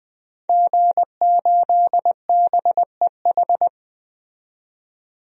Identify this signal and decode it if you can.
Morse code, heard as a keyed tone, about 20 wpm, 700 Hz. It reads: G8BEH
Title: DXCC-100 Call Signs - 5 Characters - 20wpm